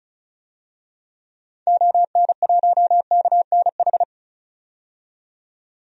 Morse code, heard as a keyed tone, about 36 wpm, 700 Hz. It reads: ON1KNH
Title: DXCC-100 Call Signs - 6 Characters - 35wpm